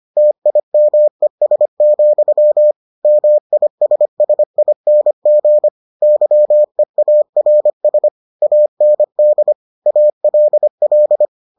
Morse code, heard as a keyed tone, about 25 wpm, 600 Hz. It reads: TIMES, MISSING YEARS AND ALL